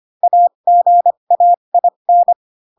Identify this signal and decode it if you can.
Morse code, heard as a keyed tone, about 25 wpm, spaced as for 22 wpm, 700 Hz. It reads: AGAIN
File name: Pinocchio / Pinocchio ch6 18 25 ik1hge